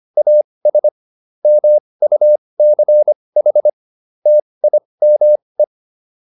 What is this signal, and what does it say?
Morse code, heard as a keyed tone, about 25 wpm, spaced as for 20 wpm, 600 Hz. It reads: AS MUCH TIME